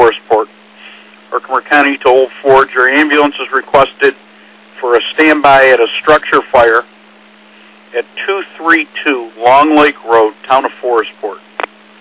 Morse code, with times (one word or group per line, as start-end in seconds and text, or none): none